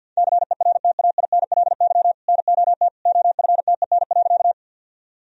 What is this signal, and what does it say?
Morse code, heard as a keyed tone, about 46 wpm, 700 Hz. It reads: CERTAINLY NOT OFTEN.
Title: Sentences from Top 1000 Words - Review Part 1 - 50wpm